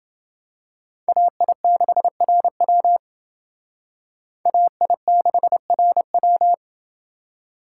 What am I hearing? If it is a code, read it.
Morse code, heard as a keyed tone, about 28 wpm, 700 Hz. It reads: AI6RW AI6RW